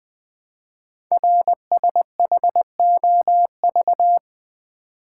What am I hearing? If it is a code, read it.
Morse code, heard as a keyed tone, about 20 wpm, 700 Hz. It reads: RSHOV
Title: Code Groups - 5 Letter:Numbers - 20wpm